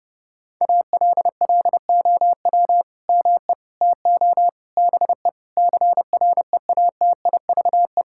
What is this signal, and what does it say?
Morse code, heard as a keyed tone, about 30 wpm, 700 Hz. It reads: ALLOW ME TO BE CREATIVE
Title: Sentences from Top 1000 Words - Review Part 5 - 30wpm